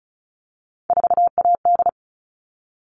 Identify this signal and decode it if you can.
Morse code, heard as a keyed tone, about 34 wpm, 700 Hz. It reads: 4AD
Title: All Cut Numbers - 3-Digit 35wpm